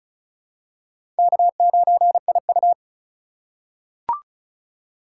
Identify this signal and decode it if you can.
Morse code, heard as a keyed tone, about 36 wpm, 690 Hz. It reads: K9IU
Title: USA Call Signs - 4 Characters - 2x Character Spacing - 35wpm